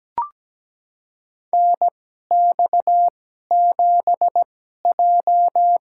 Morse code, heard as a keyed tone, about 17 wpm, spaced as for 12 wpm, 700 Hz. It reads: NX7J
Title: USA Call Signs - 4 Characters - 2x Character Spacing - 17wpm